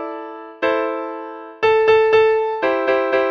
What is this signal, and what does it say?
There is no code.